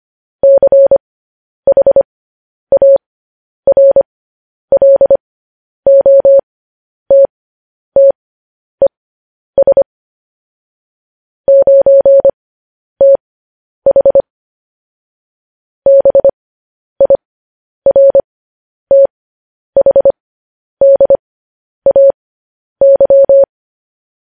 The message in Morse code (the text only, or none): CHARLOTTES 9TH BIRTHDAY